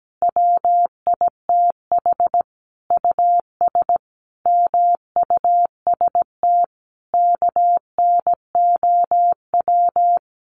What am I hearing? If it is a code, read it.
Morse code, heard as a keyed tone, about 17 wpm, 700 Hz. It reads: WITH US MUST KNOW
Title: Sentences from Top 1400 Words 17wpm